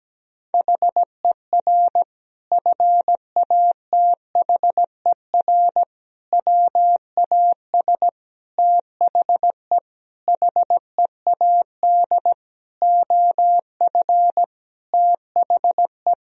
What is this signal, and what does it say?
Morse code, heard as a keyed tone, about 17 wpm, 700 Hz. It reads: HER FATHER WAS THE HEAD OF THE
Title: Sentences from Top 1000 Words - Review Part 4 - 17wpm